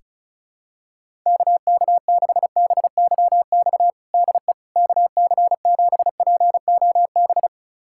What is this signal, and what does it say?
Morse code, heard as a keyed tone, about 34 wpm, 700 Hz. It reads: KK6BYX DE KC7POB
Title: USA Callsign DE Callsign - 6 Characters - 35wpm